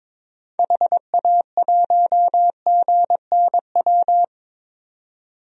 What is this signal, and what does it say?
Morse code, heard as a keyed tone, about 22 wpm, 700 Hz. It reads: HA1GNW
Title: DXCC-100 Call Signs - 6 Characters - 22wpm